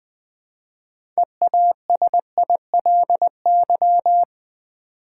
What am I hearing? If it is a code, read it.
Morse code, heard as a keyed tone, about 20 wpm, 700 Hz. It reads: EASILY
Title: Build A Word - 3 Letters At A Time - 6-Letter Words - 20wpm